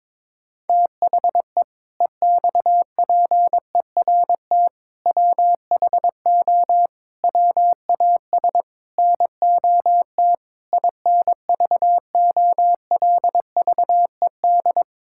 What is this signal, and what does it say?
Morse code, heard as a keyed tone, about 22 wpm, 700 Hz. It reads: THE EXPERT WHO WAS NOT INVOLVED
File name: Sentences with 7-Word Phrases - 22wpm